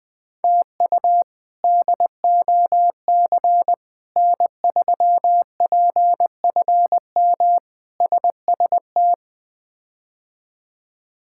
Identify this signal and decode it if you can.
Morse code, heard as a keyed tone, about 21 wpm, 700 Hz. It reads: TU DOC N3PFM SST
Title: K1USN SST QSO Practice - 20wpm